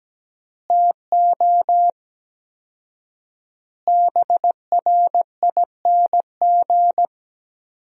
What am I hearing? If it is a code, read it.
Morse code, heard as a keyed tone, about 17 wpm, 700 Hz. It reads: TO BRING
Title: 4x Word-Spacing - Sentences from Top 500 Words - 17wpm